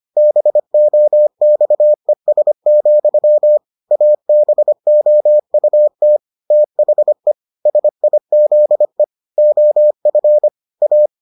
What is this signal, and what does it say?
Morse code, heard as a keyed tone, about 25 wpm, 600 Hz. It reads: BOXES, ABOUT THE SIZE OF A